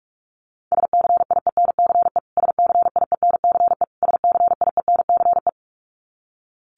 Morse code, heard as a keyed tone, about 44 wpm, 700 Hz. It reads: SCIENCE SCIENCE SCIENCE
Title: Top 500 Words Repeated 45wpm